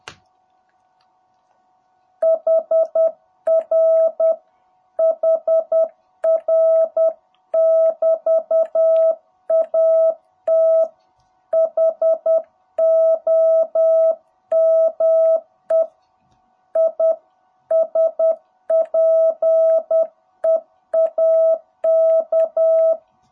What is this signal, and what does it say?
Morse code, heard as a keyed tone, about 10 wpm, 660 Hz. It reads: HR HR <BT> AT HOME I SPEAK
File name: jh8cbh / 2021cw